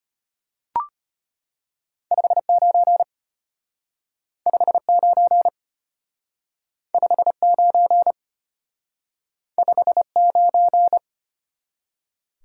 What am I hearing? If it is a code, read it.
Morse code, heard as a keyed tone, about 31 wpm, 700 Hz. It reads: 59 59 59 59
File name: Speed-Racing - Two Letter-Number Combinations - 25wpm